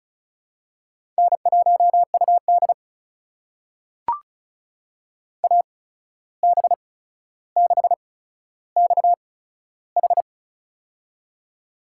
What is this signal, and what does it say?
Morse code, heard as a keyed tone, about 36 wpm, spaced as for 14 wpm, 700 Hz. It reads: N1UD AB6XH
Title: USA Call Signs - 8x Character Spacing - 35wpm